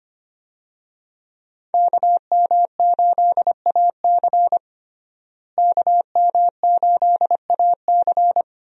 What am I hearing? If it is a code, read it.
Morse code, heard as a keyed tone, about 25 wpm, 700 Hz. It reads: KM8AC KM8AC
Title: USA Call Signs - 5 Characters - Repeated 3x - 25wpm